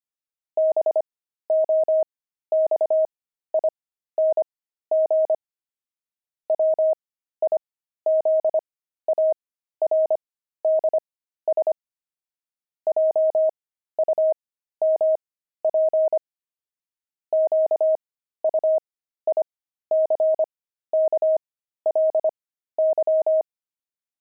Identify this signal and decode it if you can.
Morse code, heard as a keyed tone, about 25 wpm, spaced as for 13 wpm, 630 Hz. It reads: BOXING WIZARDS JUMP QUICKLY
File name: pangram-fs-13-cs-25-wizards